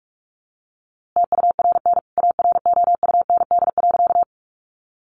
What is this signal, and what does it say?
Morse code, heard as a keyed tone, about 44 wpm, 700 Hz. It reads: TURN AROUND.
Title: A Sound Of Thunder 45wpm